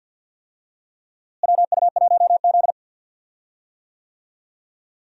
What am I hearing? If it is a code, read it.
Morse code, heard as a keyed tone, about 52 wpm, 700 Hz. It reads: WU1Z
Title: USA Call Signs - 4 Characters - Repeated 3x - 50wpm